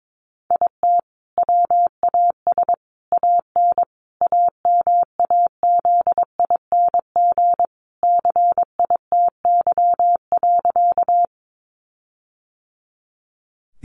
Morse code, heard as a keyed tone, about 22 wpm, 700 Hz. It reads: IT WAS AN AMAZING CITY.
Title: Sentences from Top 1500 Words 22wpm